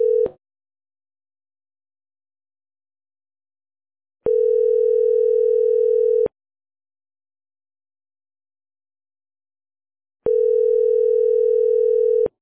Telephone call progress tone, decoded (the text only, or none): ringback tone